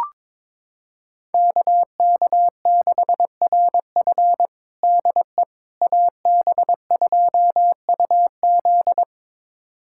Morse code, heard as a keyed tone, about 22 wpm, 700 Hz. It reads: KK6RF DE AB2UZ
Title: USA Callsign DE Callsign - 5 Characters - 22wpm